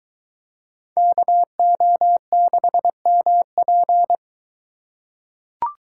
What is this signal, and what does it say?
Morse code, heard as a keyed tone, about 22 wpm, 700 Hz. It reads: KO6MP